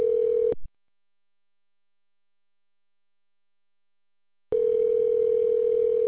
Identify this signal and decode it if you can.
Ringback tone (North America), 440 + 480 Hz.